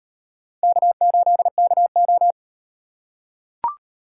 Morse code, heard as a keyed tone, about 38 wpm, 700 Hz. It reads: K8KO